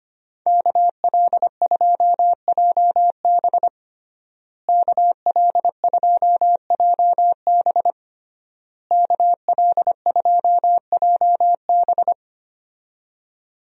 Morse code, heard as a keyed tone, about 25 wpm, 700 Hz. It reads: KL2JB KL2JB KL2JB
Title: USA Call Signs - 5 Characters - Repeated 3x - 25wpm